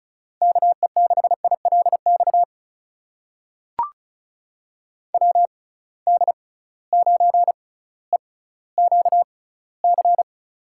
Morse code, heard as a keyed tone, about 36 wpm, spaced as for 18 wpm, 700 Hz. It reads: KE6ILX WD9EQC
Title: USA Call Signs - 6 Characters - 6x Character Spacing - 35wpm